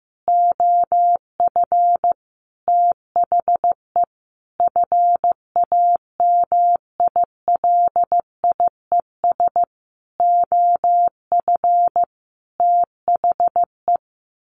Morse code, heard as a keyed tone, about 15 wpm, 700 Hz. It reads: OF THE FAMILIES OF THE
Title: A Sound Of Thunder 15wpm